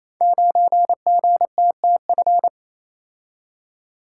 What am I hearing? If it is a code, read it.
Morse code, heard as a keyed tone, about 29 wpm, 700 Hz. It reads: 9GTTF
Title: Code Groups - 5 Letter:Numbers - 28wpm